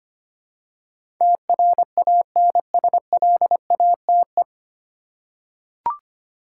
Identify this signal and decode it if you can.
Morse code, heard as a keyed tone, about 25 wpm, 700 Hz. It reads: TRANSLATE
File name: Nine Letter Words - 25wpm